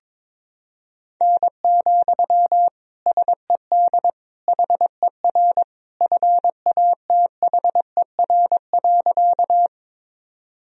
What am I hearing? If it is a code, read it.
Morse code, heard as a keyed tone, about 22 wpm, 700 Hz. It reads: N, SED HER FATHER.
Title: Sentences from Top 300 Words and Abbreviations 22wpm